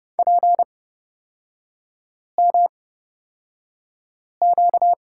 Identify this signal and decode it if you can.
Morse code, heard as a keyed tone, about 29 wpm, spaced as for 5 wpm, 700 Hz. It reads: PMQ
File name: Lesson 38e - Letter q - Three Characters 30wpm